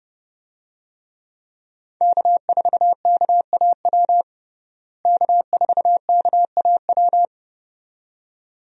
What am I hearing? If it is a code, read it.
Morse code, heard as a keyed tone, about 32 wpm, 700 Hz. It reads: K4KAW K4KAW
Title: USA Call Signs - 5 Characters - Repeated 2x - 30wpm